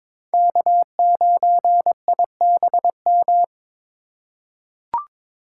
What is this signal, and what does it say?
Morse code, heard as a keyed tone, about 22 wpm, 700 Hz. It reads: K9IBM
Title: K1USN SST QSO Practice - 22wpm